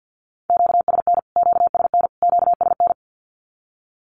Morse code, heard as a keyed tone, about 48 wpm, 700 Hz. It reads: QSN QSN QSN
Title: Q-Codes Repeated 3x 50wpm